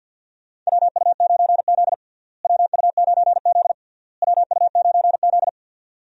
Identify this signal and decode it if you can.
Morse code, heard as a keyed tone, about 53 wpm, 700 Hz. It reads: WU9Z WU9Z WU9Z
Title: USA Call Signs - 4 Characters - Repeated 3x - 50wpm